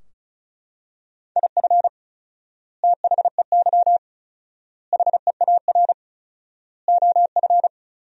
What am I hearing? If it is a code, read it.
Morse code, heard as a keyed tone, about 36 wpm, 700 Hz. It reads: IF THEY HEAR OF
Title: 4x Word-Spacing - Sentences from Top 400 Words - 35wpm